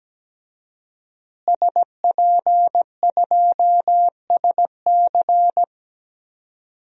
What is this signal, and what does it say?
Morse code, heard as a keyed tone, about 17 wpm, 700 Hz. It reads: SP2SC
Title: DXCC-100 Call Signs - 5 Characters - Repeated 3x - 17wpm